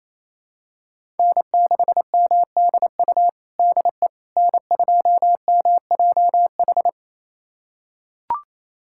Morse code, heard as a keyed tone, about 28 wpm, 700 Hz. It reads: N6MDU DE N2MJH
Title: USA Callsign DE Callsign - 5 Characters - 28wpm